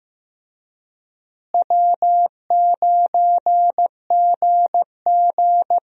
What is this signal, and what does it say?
Morse code, heard as a keyed tone, about 15 wpm, 700 Hz. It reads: W9GG
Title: USA Call Signs - 4 Characters - Repeated 3x - 15wpm